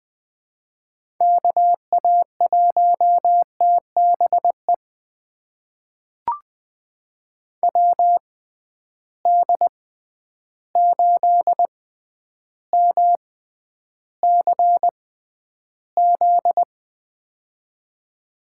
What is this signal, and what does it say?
Morse code, heard as a keyed tone, about 20 wpm, spaced as for 10 wpm, 700 Hz. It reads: KA1TBE WD8MCZ
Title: USA Call Signs - 6 Characters - 6x Character Spacing - 20wpm